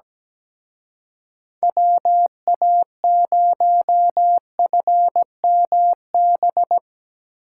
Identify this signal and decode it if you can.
Morse code, heard as a keyed tone, about 19 wpm, 700 Hz. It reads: WA0FMB
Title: Speed Racing - USA 2x3 Call Signs 17wpm